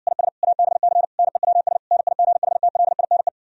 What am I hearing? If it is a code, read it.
Morse code, heard as a keyed tone, about 69 wpm, 700 Hz. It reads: IS ABC NEWS NIGHTLINE